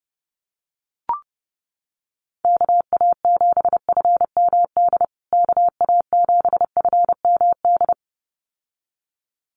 Morse code, heard as a keyed tone, about 30 wpm, 700 Hz. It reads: KA7FMD KA7FMD
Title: USA Call Signs Repeated 2x 30wpm